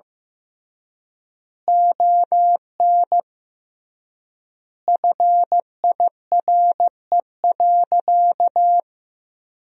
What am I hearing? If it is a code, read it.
Morse code, heard as a keyed tone, about 15 wpm, 700 Hz. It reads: ON FIRE.
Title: 3x Word-Spacing - Sentences from Top 400 Words - 15wpm